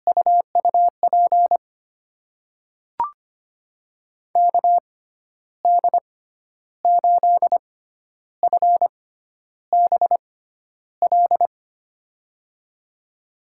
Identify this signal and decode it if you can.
Morse code, heard as a keyed tone, about 25 wpm, spaced as for 11 wpm, 700 Hz. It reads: UUP KD8FBL
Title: USA Call Signs - 6 Characters - 6x Character Spacing - 25wpm